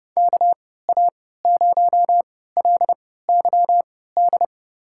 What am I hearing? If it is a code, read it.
Morse code, heard as a keyed tone, about 32 wpm, spaced as for 17 wpm, 700 Hz. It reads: KA0LYD
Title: USA Call Signs - 6 Characters - 3x Character Spacing - 30wpm